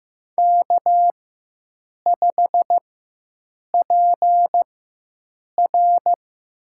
Morse code, heard as a keyed tone, about 15 wpm, spaced as for 7 wpm, 700 Hz. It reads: K5PR
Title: USA Call Signs - 4 Characters - 4x Character Spacing - 15wpm